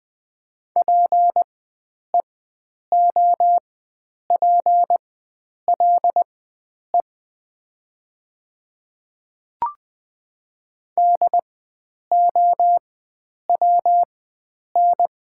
Morse code, heard as a keyed tone, about 20 wpm, spaced as for 9 wpm, 700 Hz. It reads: PEOPLE DOWN
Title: 4x Character Spacing - Top 100 Words - 20wpm